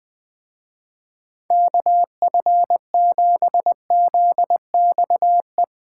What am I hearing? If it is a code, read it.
Morse code, heard as a keyed tone, about 20 wpm, 700 Hz. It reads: KF7ZXE
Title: Speed Racing - USA 2x3 Call Signs 20wpm